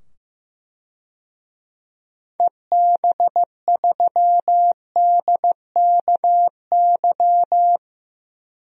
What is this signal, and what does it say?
Morse code, heard as a keyed tone, about 15 wpm, 700 Hz. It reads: EB3DKY
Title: DXCC-100 Call Signs - 6 Characters - 15wpm